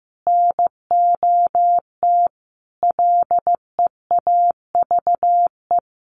Morse code, heard as a keyed tone, about 14 wpm, 700 Hz. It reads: NOT LEAVE
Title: Sentences from Top 1500 Words 15wpm